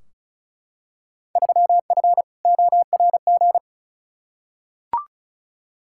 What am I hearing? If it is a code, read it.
Morse code, heard as a keyed tone, about 36 wpm, 700 Hz. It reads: 3F ORG
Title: Field Day Exchanges - 35wpm